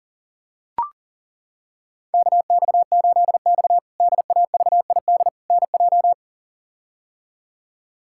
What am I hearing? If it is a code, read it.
Morse code, heard as a keyed tone, about 40 wpm, 700 Hz. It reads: KX8X DAVID NJ
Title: K1USN SST QSO Practice - 40wpm